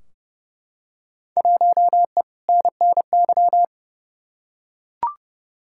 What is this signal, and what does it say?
Morse code, heard as a keyed tone, about 30 wpm, 700 Hz. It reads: 1E NNY
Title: Field Day Exchanges - 30wpm